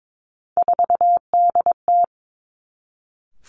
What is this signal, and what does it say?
Morse code, heard as a keyed tone, about 22 wpm, 700 Hz. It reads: 4DT
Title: Three Challenging Letter-Number Combinations 22wpm